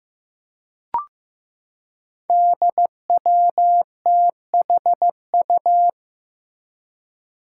Morse code, heard as a keyed tone, about 15 wpm, 700 Hz. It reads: DWTHU
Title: Five Challenging Letter-Number Combinations 15wpm